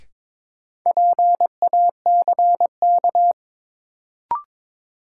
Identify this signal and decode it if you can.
Morse code, heard as a keyed tone, about 22 wpm, 700 Hz. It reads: PACK
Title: Sentences from Top 1300 Words - New Word Drill - 22wpm